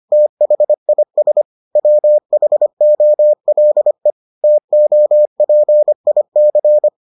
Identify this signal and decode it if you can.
Morse code, heard as a keyed tone, about 25 wpm, 600 Hz. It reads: THIS WHOLE TOPIC